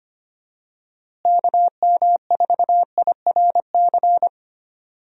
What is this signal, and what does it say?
Morse code, heard as a keyed tone, about 25 wpm, 700 Hz. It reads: KM4IRC
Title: USA Call Signs - 6 Characters - Repeated 3x - 25wpm